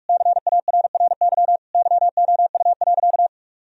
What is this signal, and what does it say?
Morse code, heard as a keyed tone, about 42 wpm, 700 Hz. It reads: CARRY YOU.
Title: Sentences from Top 1000 Words - Review Part 1 - 45wpm